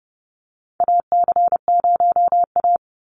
Morse code, heard as a keyed tone, about 30 wpm, 700 Hz. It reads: AC0A
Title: NA SOTA Chaser Callsigns 30wpm